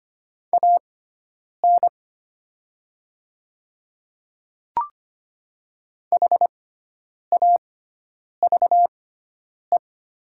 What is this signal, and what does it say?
Morse code, heard as a keyed tone, about 25 wpm, spaced as for 9 wpm, 700 Hz. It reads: AN HAVE
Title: 6x Character Spacing - Top 100 Words - 25wpm